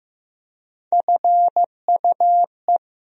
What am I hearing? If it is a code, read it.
Morse code, heard as a keyed tone, about 15 wpm, 700 Hz. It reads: FUE
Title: USA Call Sign Postfixes - 3 Letters 15wpm